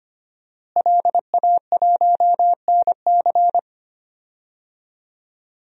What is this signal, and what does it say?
Morse code, heard as a keyed tone, about 25 wpm, 700 Hz. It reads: LA1NC
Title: DXCC-100 Call Signs - 5 Characters - Repeated 2x - 25wpm